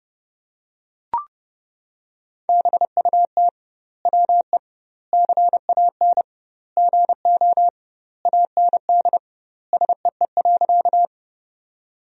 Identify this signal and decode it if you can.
Morse code, heard as a keyed tone, about 32 wpm, 700 Hz. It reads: BUT WE CAN GO AND SEE.
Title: 2x Word-Spacing - Sentences from Top 100 Words - 30wpm